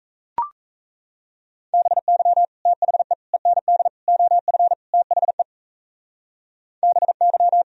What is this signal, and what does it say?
Morse code, heard as a keyed tone, about 36 wpm, 700 Hz. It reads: BY THE END OF THE BY